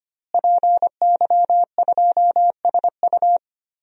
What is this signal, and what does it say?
Morse code, heard as a keyed tone, about 25 wpm, 700 Hz. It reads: PY2SU
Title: DXCC-100 Call Signs - 5 Characters - Repeated 2x - 25wpm